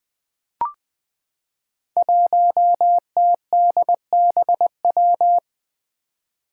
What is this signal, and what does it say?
Morse code, heard as a keyed tone, about 20 wpm, 700 Hz. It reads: 1TDBW
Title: Five Challenging Letter-Number Combinations 20wpm